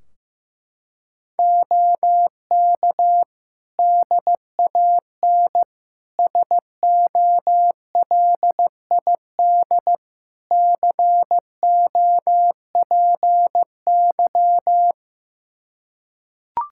Morse code, heard as a keyed tone, about 15 wpm, 700 Hz. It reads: OK DAN SOLID COPY
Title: Practice QSO - Exchange 3 - More Info - 15wpm